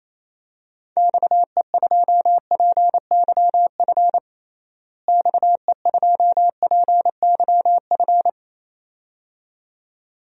Morse code, heard as a keyed tone, about 28 wpm, 700 Hz. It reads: XE2PYF XE2PYF
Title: DXCC-100 Call Signs - 6 Characters - Repeated 2x - 28wpm